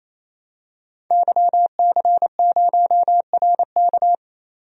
Morse code, heard as a keyed tone, about 28 wpm, 700 Hz. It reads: YC0RK
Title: DXCC-100 Call Signs - 5 Characters - 28wpm